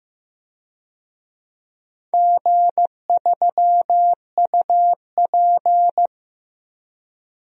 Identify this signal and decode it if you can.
Morse code, heard as a keyed tone, about 15 wpm, 700 Hz. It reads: G3UP
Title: DXCC-100 Call Signs - 4 Characters - Repeated 2x - 15wpm